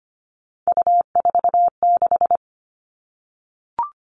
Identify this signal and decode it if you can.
Morse code, heard as a keyed tone, about 25 wpm, 700 Hz. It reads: U46
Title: Three Challenging Letter-Number Combinations 25wpm